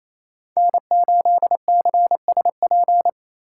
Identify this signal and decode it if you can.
Morse code, heard as a keyed tone, about 28 wpm, 700 Hz. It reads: N8CSP
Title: USA Call Signs - 5 Characters - 2x Character Spacing - 28wpm